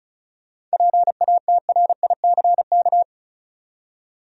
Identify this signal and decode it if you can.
Morse code, heard as a keyed tone, about 36 wpm, 700 Hz. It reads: PATRICK
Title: USA Common Names - 35wpm